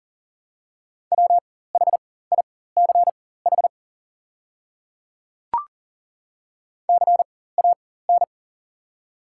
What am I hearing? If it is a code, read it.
Morse code, heard as a keyed tone, about 40 wpm, spaced as for 19 wpm, 700 Hz. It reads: WHICH CAN
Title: 4x Character Spacing - Top 200 Words - 40wpm